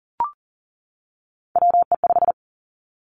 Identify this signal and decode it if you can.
Morse code, heard as a keyed tone, about 42 wpm, 700 Hz. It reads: WE5
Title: Three Challenging Letter-Number Combinations 40wpm